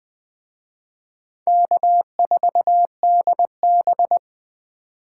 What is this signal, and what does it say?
Morse code, heard as a keyed tone, about 20 wpm, 700 Hz. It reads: K4DB
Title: Speed Racing - USA 1x2 Call Signs 17wpm